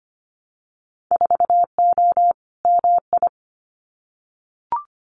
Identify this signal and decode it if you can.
Morse code, heard as a keyed tone, about 25 wpm, 700 Hz. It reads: 4O MI
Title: Winter Field Day Exchanges Encore 25wpm